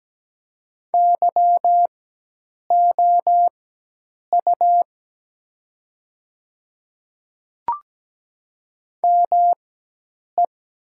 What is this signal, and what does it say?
Morse code, heard as a keyed tone, about 17 wpm, spaced as for 8 wpm, 700 Hz. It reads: YOU ME
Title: 4x Character Spacing - Top 200 Words - 17wpm